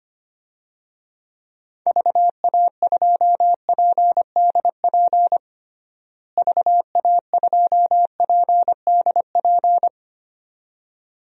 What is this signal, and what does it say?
Morse code, heard as a keyed tone, about 25 wpm, 700 Hz. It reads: VA2PDP VA2PDP
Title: DXCC-100 Call Signs - 6 Characters - Repeated 2x - 25wpm